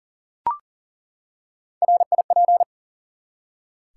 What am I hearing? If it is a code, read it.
Morse code, heard as a keyed tone, about 38 wpm, 700 Hz. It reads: RIP
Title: Three Letter Words - 40wpm